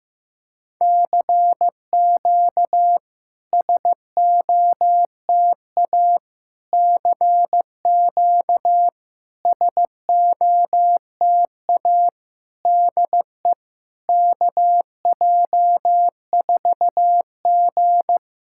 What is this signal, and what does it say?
Morse code, heard as a keyed tone, about 15 wpm, 700 Hz. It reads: CQ SOTA CQ SOTA DE KJ4G
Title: SOTA QSO Practice 15wpm